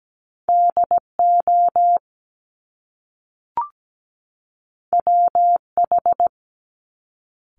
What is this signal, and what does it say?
Morse code, heard as a keyed tone, about 17 wpm, 700 Hz. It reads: DO WH
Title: Top Two Letter Combinations in Words 17wpm